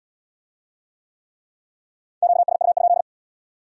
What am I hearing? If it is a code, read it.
Morse code, heard as a keyed tone, about 72 wpm, 700 Hz. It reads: QSU?